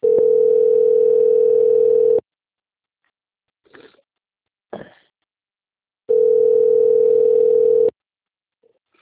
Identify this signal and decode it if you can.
Ringback tone (North America), 440 + 480 Hz.